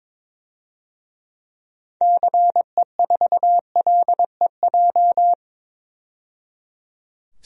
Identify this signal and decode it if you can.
Morse code, heard as a keyed tone, about 22 wpm, 700 Hz. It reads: CE4LEJ
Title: DXCC-100 Call Signs - 6 Characters - 22wpm